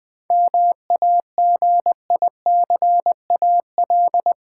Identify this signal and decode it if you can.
Morse code, heard as a keyed tone, about 20 wpm, 700 Hz. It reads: MAGICAL
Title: 7 Letter Words - First 3 Letters Given - 20wpm